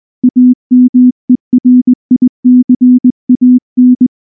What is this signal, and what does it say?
Morse code, heard as a keyed tone, about 19 wpm, 250 Hz. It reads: AMERICAN
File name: morse / morse-training-20WPM-8